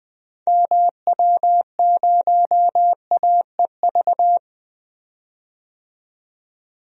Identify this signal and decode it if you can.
Morse code, heard as a keyed tone, about 20 wpm, 700 Hz. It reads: MW0AEV
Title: DXCC-100 Call Signs - 6 Characters - 20wpm